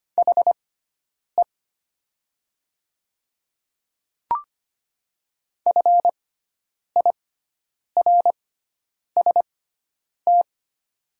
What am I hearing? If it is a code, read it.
Morse code, heard as a keyed tone, about 25 wpm, spaced as for 9 wpm, 700 Hz. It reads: HE FIRST
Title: 6x Character Spacing - Top 300 Words - 25wpm